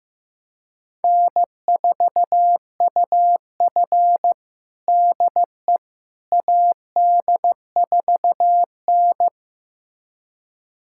Morse code, heard as a keyed tone, about 15 wpm, 700 Hz. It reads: N4UF DE AD4N
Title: USA Callsign DE Callsign - 4 Characters - 15wpm